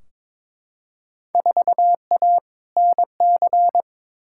Morse code, heard as a keyed tone, about 22 wpm, 700 Hz. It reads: 4A NC
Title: Field Day Exchanges Encore - 22wpm